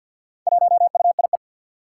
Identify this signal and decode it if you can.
Morse code, heard as a keyed tone, about 53 wpm, 700 Hz. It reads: 1UIE
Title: Four Challenging Letter-Number Combinations 50wpm